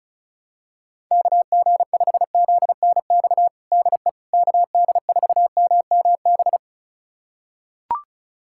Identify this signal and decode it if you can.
Morse code, heard as a keyed tone, about 34 wpm, 700 Hz. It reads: KG5ZNX DE KD4MMB
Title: USA Callsign DE Callsign 35wpm